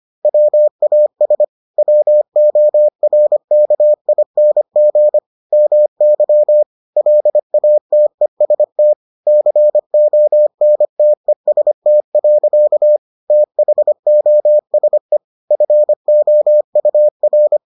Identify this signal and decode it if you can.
Morse code, heard as a keyed tone, about 25 wpm, 600 Hz. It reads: WAS WORKING MY LATEST CONTEST. THOSE FOUR